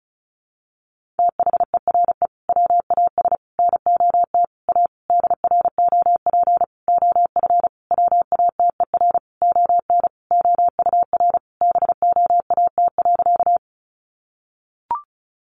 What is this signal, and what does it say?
Morse code, heard as a keyed tone, about 34 wpm, 700 Hz. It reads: THERE WAS NOT A DROP OF WATER ON OUR BOAT.
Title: Sentences from Top 1400 Words 35wpm